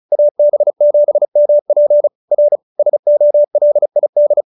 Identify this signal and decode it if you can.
Morse code, heard as a keyed tone, about 34 wpm, 590 Hz. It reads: AB7MP R SOLID